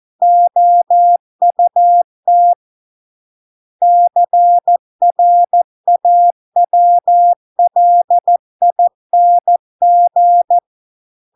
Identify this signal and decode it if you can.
Morse code, heard as a keyed tone, about 14 wpm, 700 Hz. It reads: OUT CRAWLING